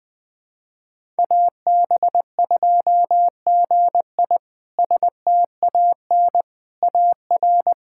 Morse code, heard as a keyed tone, about 20 wpm, 700 Hz. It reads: AB2GI STAN AR
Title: K1USN SST QSO Practice - 20wpm